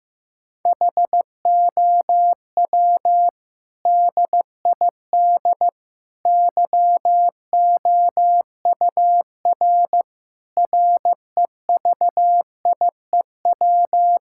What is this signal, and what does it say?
Morse code, heard as a keyed tone, about 15 wpm, 700 Hz. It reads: HOW DID YOUR REVIEW
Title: Sentences from Top 1000 Words - Review Part 6 - 15wpm